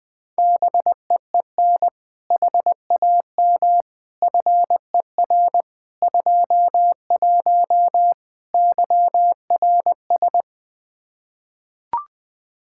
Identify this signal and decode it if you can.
Morse code, heard as a keyed tone, about 20 wpm, 700 Hz. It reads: BEEN HAM FER 21 YRS
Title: Practice QSO - Exchange 3 - More Info - 20wpm